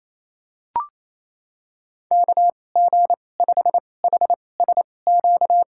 Morse code, heard as a keyed tone, about 28 wpm, spaced as for 20 wpm, 700 Hz. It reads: KG5HSQ
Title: USA Call Signs - 6 Characters - 2x Character Spacing - 28wpm